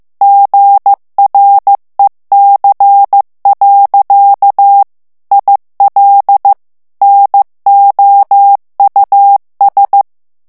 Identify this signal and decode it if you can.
Morse code, heard as a keyed tone, about 15 wpm, 800 Hz. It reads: GREC. IL NOUS
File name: FRANCAIS TEXTE CLAIR 2 VITESSE 14 MOTS MINUTES